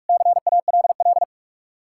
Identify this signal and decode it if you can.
Morse code, heard as a keyed tone, about 60 wpm, 700 Hz. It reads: CALL